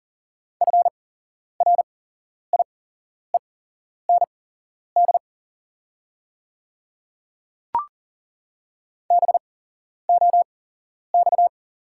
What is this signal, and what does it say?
Morse code, heard as a keyed tone, about 40 wpm, spaced as for 11 wpm, 690 Hz. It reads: FRIEND BOX
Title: 8x Character Spacing - Top 500 Words - 40wpm